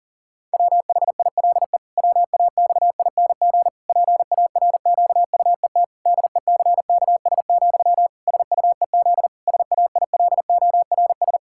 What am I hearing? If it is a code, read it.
Morse code, heard as a keyed tone, about 40 wpm, 700 Hz. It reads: WHILE WAXING PARQUET DECKS, SUEZ SAILORS
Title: Pangram Challenge v2 - 40wpm